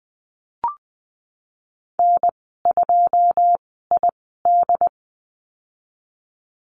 Morse code, heard as a keyed tone, about 20 wpm, spaced as for 14 wpm, 700 Hz. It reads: N2ID
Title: 213 Lesson 34g - Number 8 - Call Signs 20wpm